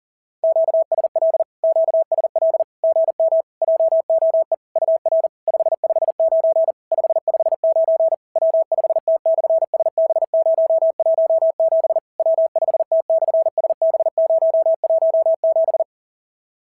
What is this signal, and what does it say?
Morse code, heard as a keyed tone, about 40 wpm, 650 Hz. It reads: QSL QSL GM JOE UR 559 559 W5T/SB017 W5T/SB017
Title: SOTA QSO Practice 40wpm